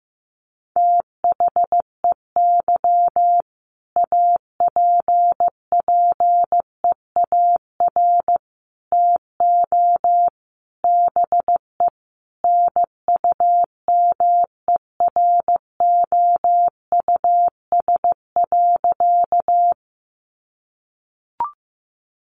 Morse code, heard as a keyed tone, about 15 wpm, 700 Hz. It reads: THEY APPEAR TO BE NUMEROUS.
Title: Sentences from Top 1400 Words 15wpm